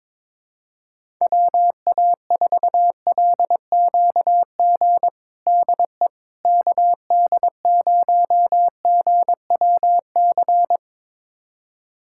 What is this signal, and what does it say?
Morse code, heard as a keyed tone, about 22 wpm, 700 Hz. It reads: WA4LQG DE KD0GWC
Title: USA Callsign DE Callsign - 6 Characters - 22wpm